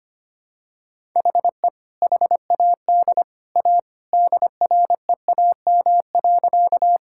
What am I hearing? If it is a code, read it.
Morse code, heard as a keyed tone, about 25 wpm, 700 Hz. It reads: HE HAD A DREAM.
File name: practice / 25wpm / Sentences with 2-Word Phrases - 25wpm